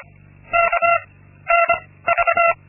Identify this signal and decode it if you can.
Morse code, heard as a keyed tone, about 25 wpm, 680 Hz. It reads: K N V
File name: Linksmakalnis-2